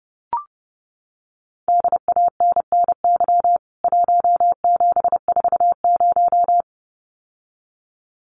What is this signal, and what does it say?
Morse code, heard as a keyed tone, about 30 wpm, 700 Hz. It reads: DANNY 1740
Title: CWT Exchange 30wpm